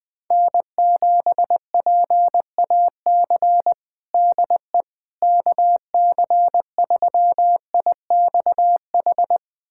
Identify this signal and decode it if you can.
Morse code, heard as a keyed tone, about 20 wpm, 700 Hz. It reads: N7PAC DE KC3IXH
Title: USA Callsign DE Callsign 20wpm